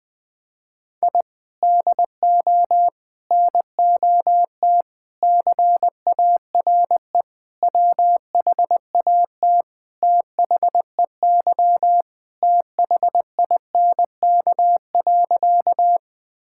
Morse code, heard as a keyed tone, about 20 wpm, 700 Hz. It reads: I DO NOT CARE WHAT THEY THINK.
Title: Sentences with 5-Word Phrases - Spoken Beginning - 20wpm